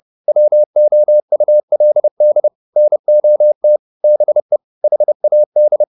Morse code, heard as a keyed tone, about 30 wpm, 600 Hz. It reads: WOULD NOT BE HAD
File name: the old man and the sea / the old man and the sea 1 30wpm